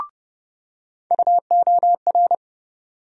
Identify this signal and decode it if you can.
Morse code, heard as a keyed tone, about 31 wpm, 700 Hz. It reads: UOR
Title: USA Call Sign Postfixes - 3 Letters 30wpm